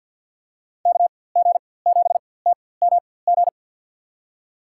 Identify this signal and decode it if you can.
Morse code, heard as a keyed tone, about 49 wpm, spaced as for 23 wpm, 700 Hz. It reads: KG7TMG
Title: USA Call Signs - 6 Characters - 4x Character Spacing - 50wpm